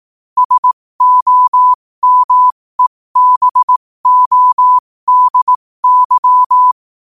Morse code, heard as a keyed tone, about 18 wpm, spaced as for 15 wpm, 1000 Hz. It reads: SOMEBODY
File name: qotdfiles / 15wpm